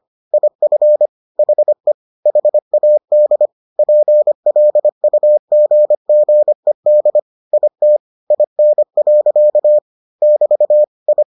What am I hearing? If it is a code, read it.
Morse code, heard as a keyed tone, about 25 wpm, 600 Hz. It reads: IF HE HAD PLUGGED IT IN. <BT> I